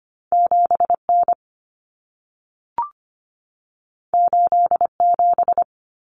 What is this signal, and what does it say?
Morse code, heard as a keyed tone, about 25 wpm, 700 Hz. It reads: 7N 87
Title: Common Cut Numbers - 2-Digit 25wpm